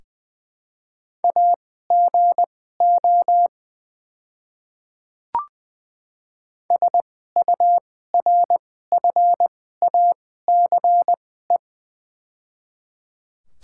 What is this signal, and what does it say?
Morse code, heard as a keyed tone, about 20 wpm, spaced as for 14 wpm, 700 Hz. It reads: AGO SURFACE
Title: 2x Character Spacing - Top 500 Words - 20wpm